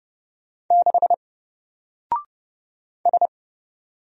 Morse code, heard as a keyed tone, about 30 wpm, 700 Hz. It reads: 6 S
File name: Single-Letter-Number-30wpm